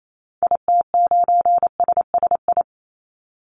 Morse code, heard as a keyed tone, about 28 wpm, 700 Hz. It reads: IT9SSI
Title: Call Signs Repeated 2x - 28wpm